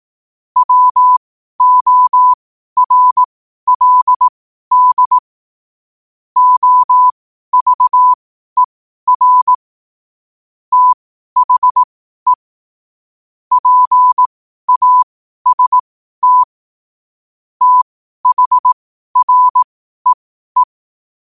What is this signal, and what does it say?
Morse code, heard as a keyed tone, about 18 wpm, spaced as for 12 wpm, 1000 Hz. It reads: WORLD OVER THE PAST THREE
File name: qotdfiles / 12wpm